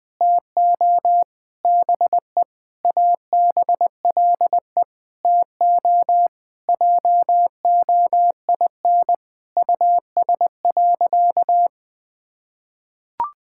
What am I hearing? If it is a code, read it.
Morse code, heard as a keyed tone, about 20 wpm, 700 Hz. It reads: TO BE ABLE TO JOIN US.
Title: Sentences with 5-Word Phrases - Spoken Beginning - 20wpm